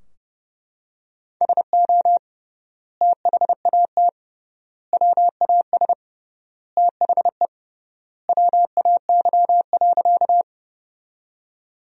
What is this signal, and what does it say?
Morse code, heard as a keyed tone, about 32 wpm, 700 Hz. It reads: SO THAT WAS THE WAY.
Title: 3x Word-Spacing - Sentences from Top 100 Words - 30wpm